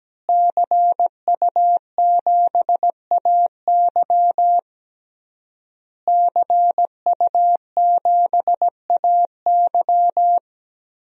Morse code, heard as a keyed tone, about 17 wpm, 700 Hz. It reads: CU7AY CU7AY
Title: DXCC-100 Call Signs - 5 Characters - Repeated 3x - 17wpm